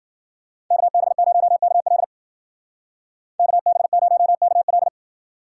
Morse code, heard as a keyed tone, about 52 wpm, 700 Hz. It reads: KB0KL KB0KL